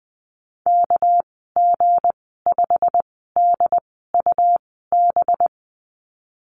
Morse code, heard as a keyed tone, about 20 wpm, spaced as for 14 wpm, 700 Hz. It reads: KG5DUB
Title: 238 Lesson 38g - Letter q - Call Signs 20wpm